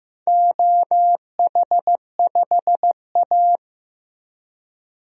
Morse code, heard as a keyed tone, about 15 wpm, 700 Hz. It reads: OH5A
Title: DXCC-100 Call Signs - 4 Characters - Repeated 3x - 15wpm